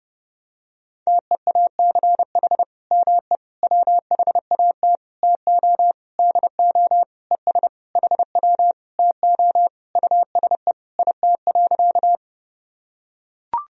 Morse code, heard as a keyed tone, about 30 wpm, 700 Hz. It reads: TEACH ME WHAT TO DO ES HW TO USE IT.
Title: Sentences from Top 500 Words and Abbreviations 30wpm